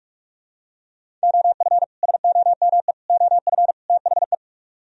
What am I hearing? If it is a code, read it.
Morse code, heard as a keyed tone, about 48 wpm, 700 Hz. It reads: OF SOME OF THE